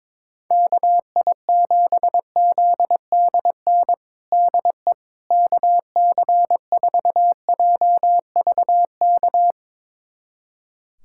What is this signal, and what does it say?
Morse code, heard as a keyed tone, about 22 wpm, 700 Hz. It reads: KI7ZDN DE KC4JVK
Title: USA Callsign DE Callsign - 6 Characters - 22wpm